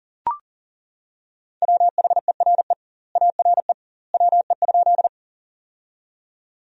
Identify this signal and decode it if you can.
Morse code, heard as a keyed tone, about 40 wpm, 700 Hz. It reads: WHERE ARE WE?
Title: 2x Word-Spacing - Sentences from Top 200 Words - 40wpm